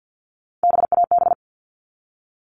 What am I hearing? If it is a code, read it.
Morse code, heard as a keyed tone, about 50 wpm, 700 Hz. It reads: BAB
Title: All Cut Numbers - 3-Digit 50wpm